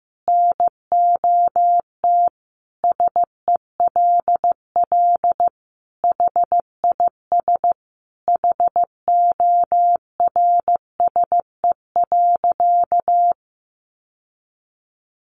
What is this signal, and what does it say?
Morse code, heard as a keyed tone, about 15 wpm, 700 Hz. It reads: NOT SELL HIS HORSE.
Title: Sentences from Top 1100 Words 15wpm